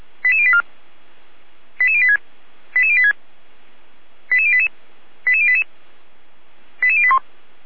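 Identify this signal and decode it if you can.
Five-tone selective call (ZVEI), heard as quick runs of five tones, tones of about 70 ms. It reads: 88084 88086 88086 88088 88088 88081
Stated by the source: ZVEI five-tone selective call received on an SDR.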